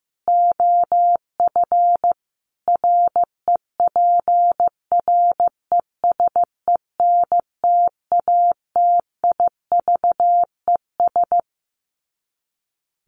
Morse code, heard as a keyed tone, about 15 wpm, 700 Hz. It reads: OF REPRESENTATIVES
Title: Sets of 5 Words Encore 15wpm